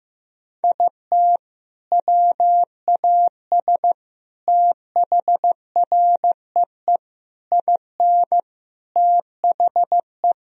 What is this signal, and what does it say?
Morse code, heard as a keyed tone, about 15 wpm, 700 Hz. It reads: IT WAS THREE IN THE
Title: Sentences from Top 1000 Words - Review Part 1 - 15wpm